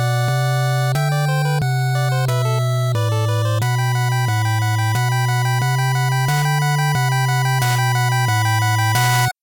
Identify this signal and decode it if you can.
There is no code.